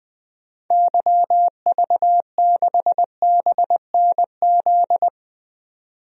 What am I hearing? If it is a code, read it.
Morse code, heard as a keyed tone, about 20 wpm, 700 Hz. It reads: YV6BNZ
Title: DXCC-100 Call Signs - 6 Characters - 20wpm